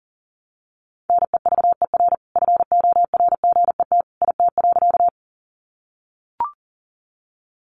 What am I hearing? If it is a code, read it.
Morse code, heard as a keyed tone, about 40 wpm, 700 Hz. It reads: NEVER FORGET IT.
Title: Sentences from Top 1200 Words 40wpm